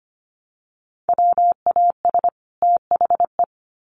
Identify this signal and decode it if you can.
Morse code, heard as a keyed tone, about 25 wpm, 700 Hz. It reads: WAS THE
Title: Sets of 2 Words 25wpm